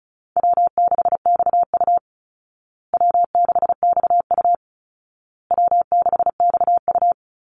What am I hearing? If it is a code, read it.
Morse code, heard as a keyed tone, about 34 wpm, 700 Hz. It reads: W6XU W6XU W6XU
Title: Easy Call Signs Repeated 3x - 35wpm